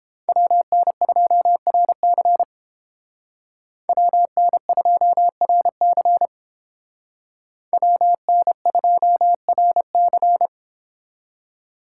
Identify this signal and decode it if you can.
Morse code, heard as a keyed tone, about 30 wpm, 700 Hz. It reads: WN2RC WN2RC WN2RC